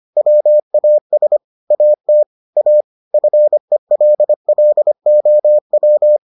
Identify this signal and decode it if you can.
Morse code, heard as a keyed tone, about 25 wpm, 600 Hz. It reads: WAS AT A FELLOW